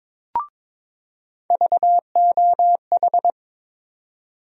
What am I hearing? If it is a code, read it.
Morse code, heard as a keyed tone, about 22 wpm, 700 Hz. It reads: VOH
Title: USA Call Sign Postfixes - 3 Letters 22wpm